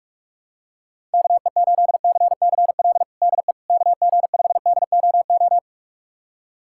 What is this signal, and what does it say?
Morse code, heard as a keyed tone, about 47 wpm, 700 Hz. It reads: KE8CCL DE KG5DOO
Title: USA Callsign DE Callsign - 6 Characters - 45wpm